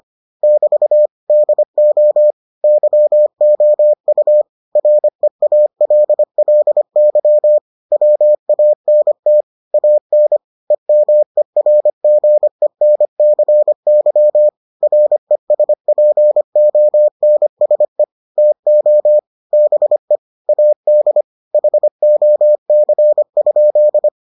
Morse code, heard as a keyed tone, about 25 wpm, 600 Hz. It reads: <BT> DO YOU REALLY WANT AN EMERGENCY RESPONSE TO BE AD HOC?